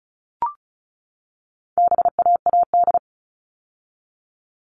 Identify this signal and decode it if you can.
Morse code, heard as a keyed tone, about 34 wpm, 700 Hz. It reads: BAAD
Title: All Cut Numbers - 4-Digit 35wpm